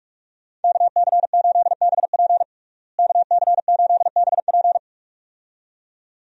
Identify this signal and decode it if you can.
Morse code, heard as a keyed tone, about 46 wpm, 700 Hz. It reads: KC8BP KC8BP
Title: USA Call Signs - 5 Characters - Repeated 2x - 45wpm